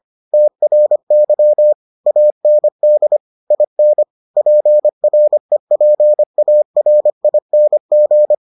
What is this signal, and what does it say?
Morse code, heard as a keyed tone, about 25 wpm, 600 Hz. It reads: TRY AND IN PREPARING